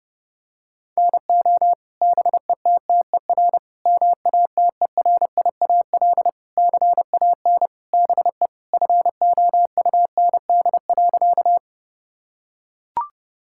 Morse code, heard as a keyed tone, about 32 wpm, 700 Hz. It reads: NO BETTER MATERIAL CAN BE FOUND.
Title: Sentences from Top 1000 Words - Review Part 5 - 30wpm